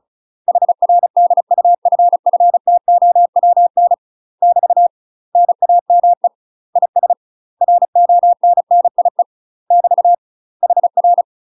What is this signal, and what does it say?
Morse code, heard as a keyed tone, about 34 wpm, 700 Hz. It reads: HR DUFFTOWN <BT> NAME IS RONNIE <BT> HR
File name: UK MASTER QSOs TRAINING / 35WPM / UK MASTER QSOs 0008